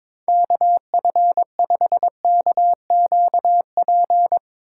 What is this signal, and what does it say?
Morse code, heard as a keyed tone, about 22 wpm, 700 Hz. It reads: KF5KQP
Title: USA Call Signs - 6 Characters - 8x Character Spacing - 22wpm